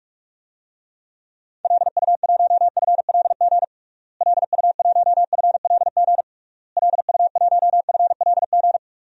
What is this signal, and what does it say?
Morse code, heard as a keyed tone, about 46 wpm, 700 Hz. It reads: LU1FLG LU1FLG LU1FLG
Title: DXCC-100 Call Signs - 6 Characters - Repeated 3x - 45wpm